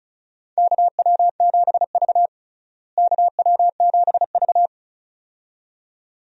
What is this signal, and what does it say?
Morse code, heard as a keyed tone, about 36 wpm, 700 Hz. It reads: KW7V KW7V
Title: USA Call Signs - 4 Characters - Repeated 2x - 35wpm